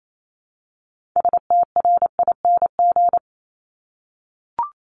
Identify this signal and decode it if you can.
Morse code, heard as a keyed tone, about 28 wpm, 700 Hz. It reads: STRING
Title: Multi-letter Phonemes 28wpm